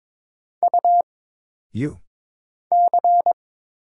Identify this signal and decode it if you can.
Morse code, heard as a keyed tone, about 23 wpm, 700 Hz. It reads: U C
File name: Single Character (Letter:Number:Punctuation:Prosign) - Mind-Melt - 22wpm